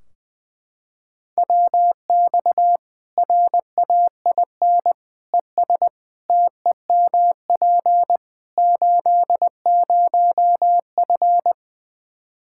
Morse code, heard as a keyed tone, about 20 wpm, 700 Hz. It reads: WX RAIN ES TEMP 80F
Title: K1USN SST QSO Practice 20wpm